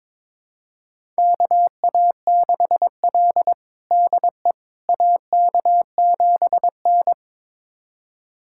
Morse code, heard as a keyed tone, about 22 wpm, 700 Hz. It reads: KA6L DE AK7N
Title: USA Callsign DE Callsign - 4 Characters - 22wpm